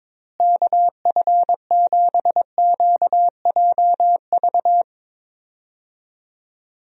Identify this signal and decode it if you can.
Morse code, heard as a keyed tone, about 22 wpm, 700 Hz. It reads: KF7QJV
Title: USA Call Signs - 6 Characters - Repeated 3x - 22wpm